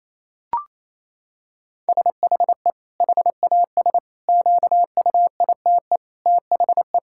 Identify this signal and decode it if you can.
Morse code, heard as a keyed tone, about 28 wpm, 700 Hz. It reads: SHE HAS QUITE THE
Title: Sentences from Top 1000 Words - Review Part 5 - 28wpm